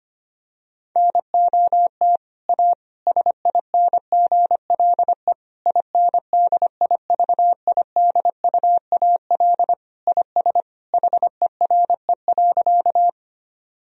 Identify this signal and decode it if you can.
Morse code, heard as a keyed tone, about 25 wpm, 700 Hz. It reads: NOT A SINGLE INDIVIDUAL IS HERE.
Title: Sentences from Top 1000 Words - Review Part 4 - 25wpm